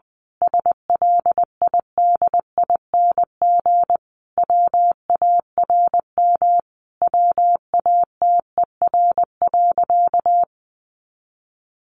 Morse code, heard as a keyed tone, about 20 wpm, 700 Hz. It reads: SLIDING WARM WATER.
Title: A Sound Of Thunder 20wpm